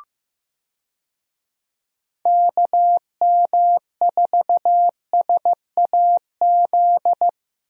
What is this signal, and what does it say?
Morse code, heard as a keyed tone, about 15 wpm, 700 Hz. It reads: KM4SAZ
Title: USA Call Signs - 6 Characters - Repeated 2x - 15wpm